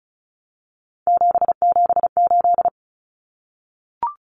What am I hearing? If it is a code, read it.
Morse code, heard as a keyed tone, about 34 wpm, 700 Hz. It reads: 778
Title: Numbers - 3-Digit 35wpm